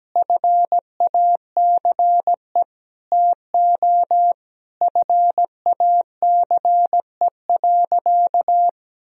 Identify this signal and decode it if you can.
Morse code, heard as a keyed tone, about 17 wpm, 700 Hz. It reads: FACE TO FACE.
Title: Sentences from Top 1000 Words - Review Part 2 - 17wpm